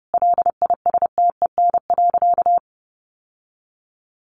Morse code, heard as a keyed tone, about 30 wpm, 700 Hz. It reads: LISTEN.
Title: Sentences from Top 1300 Words 30wpm